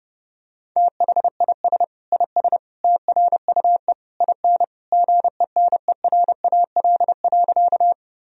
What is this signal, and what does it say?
Morse code, heard as a keyed tone, about 30 wpm, 700 Hz. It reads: THIS IS TRUE IN GENERAL.
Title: Sentences from Top 1000 Words - Review Part 3 - 30wpm